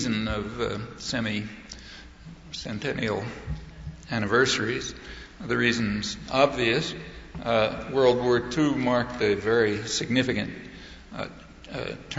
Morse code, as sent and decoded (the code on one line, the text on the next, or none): none